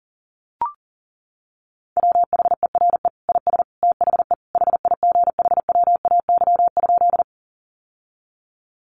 Morse code, heard as a keyed tone, about 40 wpm, 700 Hz. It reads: WHERE IS THE HIGHWAY?
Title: Sentences from Top 1400 Words 40wpm